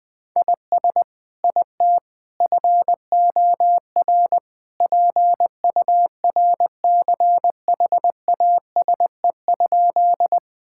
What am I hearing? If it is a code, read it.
Morse code, heard as a keyed tone, about 20 wpm, 700 Hz. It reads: IS IT FOR PURCHASE?
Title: Sentences from Top 1000 Words - Review Part 5 - 20wpm